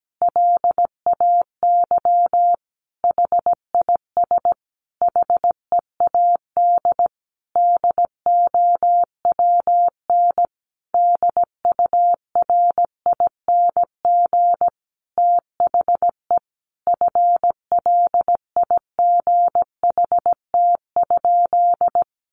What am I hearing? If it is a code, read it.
Morse code, heard as a keyed tone, about 17 wpm, 700 Hz. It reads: LAY HIS HEAD DOWN DURING THE FLIGHT?
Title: Sentences from Top 1000 Words 17wpm